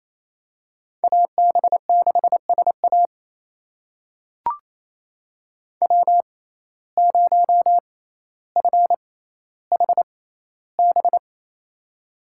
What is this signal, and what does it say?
Morse code, heard as a keyed tone, about 28 wpm, spaced as for 14 wpm, 700 Hz. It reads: AB6SA W0FHB
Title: USA Call Signs - 5 Characters - 6x Character Spacing - 28wpm